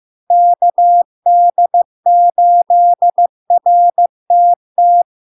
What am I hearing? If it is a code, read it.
Morse code, heard as a keyed tone, about 15 wpm, 700 Hz. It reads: KD8RTT
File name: LARK-FD-2022-Log-in-CW-15wpm-700-Hz